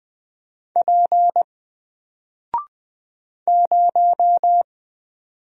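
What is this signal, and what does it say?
Morse code, heard as a keyed tone, about 20 wpm, 700 Hz. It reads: P 0
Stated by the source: Single letter and number Morse drill at 20 wpm, from N4LCD.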